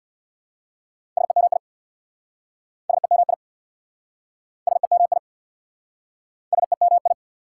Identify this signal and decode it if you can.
Morse code, heard as a keyed tone, about 68 wpm, 700 Hz. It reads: SEMI SEMI SEMI SEMI